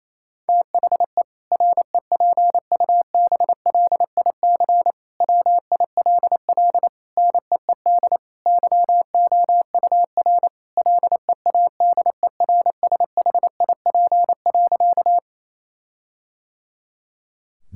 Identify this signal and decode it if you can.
Morse code, heard as a keyed tone, about 27 wpm, 700 Hz. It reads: THE REPUBLIC WILL NEED YOUR LEADERSHIP.
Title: Sentences from Top 1600 Words 28wpm